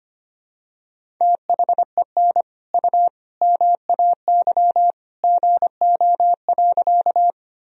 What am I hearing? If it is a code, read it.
Morse code, heard as a keyed tone, about 25 wpm, 700 Hz. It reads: THEN U MAY GO.
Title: Sentences from Top 100 Words and Abbreviations 25wpm